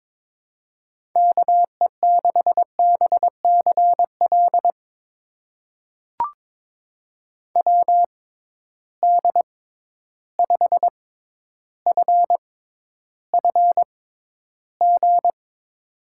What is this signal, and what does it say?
Morse code, heard as a keyed tone, about 22 wpm, spaced as for 11 wpm, 700 Hz. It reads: KE6BCL WD5FFG
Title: USA Call Signs - 6 Characters - 6x Character Spacing - 22wpm